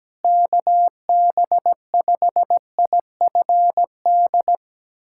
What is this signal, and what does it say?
Morse code, heard as a keyed tone, about 17 wpm, 700 Hz. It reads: KB5IFD
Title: USA Callsign DE Callsign 17wpm